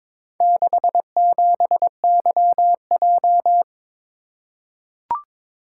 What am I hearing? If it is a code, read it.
Morse code, heard as a keyed tone, about 22 wpm, 690 Hz. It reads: 67YJ
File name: practice / 22wpm / Four Letter-Number Combinations - 22wpm